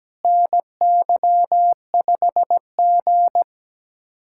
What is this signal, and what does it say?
Morse code, heard as a keyed tone, about 17 wpm, 700 Hz. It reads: NY5G
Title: USA Callsign DE Callsign - 4 Characters - 17wpm